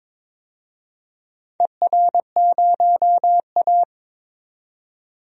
Morse code, heard as a keyed tone, about 21 wpm, 700 Hz. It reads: ER0A
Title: DXCC-100 Call Signs - 4 Characters - 22wpm